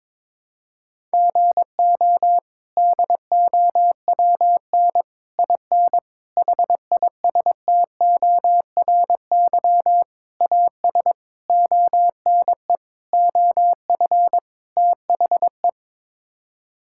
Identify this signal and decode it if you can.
Morse code, heard as a keyed tone, about 22 wpm, 700 Hz. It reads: GO DOWN IN HISTORY AS ONE OF THE
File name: practice / 22wpm / Sets of 8 Words - 22wpm